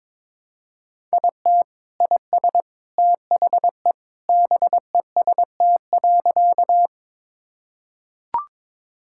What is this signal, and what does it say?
Morse code, heard as a keyed tone, about 22 wpm, 700 Hz. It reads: IT IS THE BEST.
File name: practice / 22wpm / Sentences with 3-Word Phrases - 22wpm